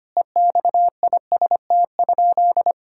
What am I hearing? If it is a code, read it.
Morse code, heard as a keyed tone, about 25 wpm, 700 Hz. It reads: EXIST?
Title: Sentences from Top 1000 Words - Review Part 5 - 25wpm